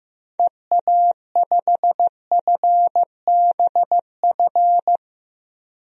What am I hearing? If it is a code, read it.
Morse code, heard as a keyed tone, about 15 wpm, 700 Hz. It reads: EA5FBF
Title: DXCC-100 Call Signs - 6 Characters - 15wpm